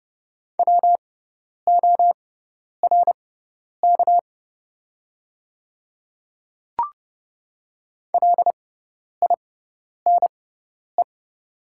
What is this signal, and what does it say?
Morse code, heard as a keyed tone, about 30 wpm, spaced as for 10 wpm, 700 Hz. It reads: WORK LINE
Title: 6x Character Spacing - Top 200 Words - 30wpm